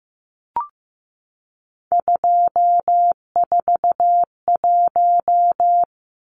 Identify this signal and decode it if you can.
Morse code, heard as a keyed tone, about 15 wpm, 700 Hz. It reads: 241
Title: Numbers - 3-Digit 15wpm